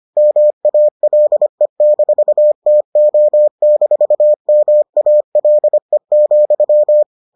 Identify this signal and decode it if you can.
Morse code, heard as a keyed tone, about 25 wpm, 600 Hz. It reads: MALE-TO-MALE,